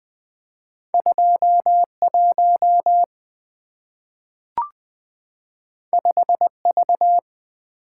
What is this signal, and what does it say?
Morse code, heard as a keyed tone, about 20 wpm, 700 Hz. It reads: 21 5V
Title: Two Letter-Number Combinations - 20wpm